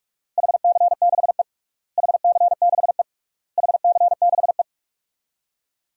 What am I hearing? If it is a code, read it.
Morse code, heard as a keyed tone, about 46 wpm, 700 Hz. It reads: HC6E HC6E HC6E
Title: DXCC-100 Call Signs - 4 Characters - Repeated 3x - 45wpm